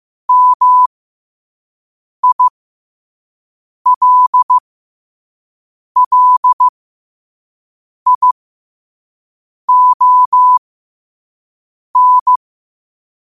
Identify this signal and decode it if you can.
Morse code, heard as a keyed tone, about 15 wpm, spaced as for 5 wpm, 1000 Hz. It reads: MILLION
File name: qotdfiles / 5wpm